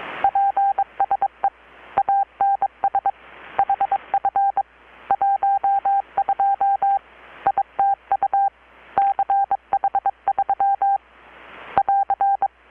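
Morse code, heard as a keyed tone, about 23 wpm, 790 Hz. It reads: PSE ANS HF 12 ITU CH3 <AR>
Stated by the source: Morse from KPH's Night of Nights XX maritime radiotelegraphy event, received in Chiapas, Mexico about 3,800 km away on a Kenwood R-600 with a grounded 10 m long-wire antenna.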